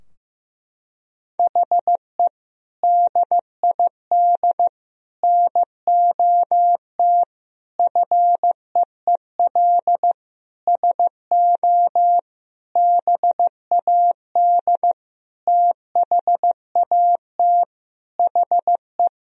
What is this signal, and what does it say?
Morse code, heard as a keyed tone, about 15 wpm, 700 Hz. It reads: HE DID NOT FEEL SO BAD THAT HE
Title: Sentences from Top 1000 Words - Review Part 4 - 15wpm